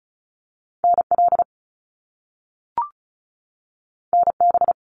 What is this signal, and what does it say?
Morse code, heard as a keyed tone, about 36 wpm, 700 Hz. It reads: NL NB
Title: Canadian Provinces and Territories 35wpm